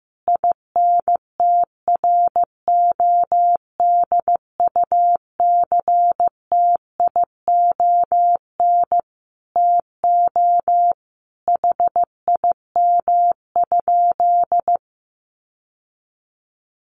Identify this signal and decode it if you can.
Morse code, heard as a keyed tone, about 15 wpm, 700 Hz. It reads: INTRODUCTION TO HIM?
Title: Sentences from Top 1000 Words 15wpm